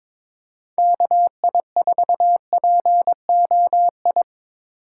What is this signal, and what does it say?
Morse code, heard as a keyed tone, about 24 wpm, 700 Hz. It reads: KI4POI
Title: Speed Racing - USA Call Signs 22wpm